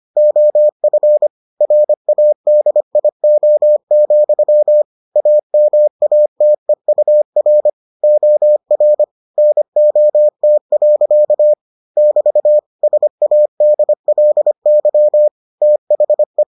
Morse code, heard as a keyed tone, about 25 wpm, 600 Hz. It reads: OF RADIO, AMATEUR OR NOT. <BT> SADLY THE